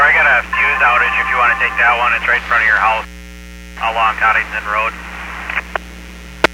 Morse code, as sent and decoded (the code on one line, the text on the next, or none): none